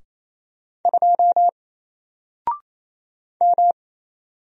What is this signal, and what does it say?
Morse code, heard as a keyed tone, about 27 wpm, 700 Hz. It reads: 2 M
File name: Single-Letter-Number-28wpm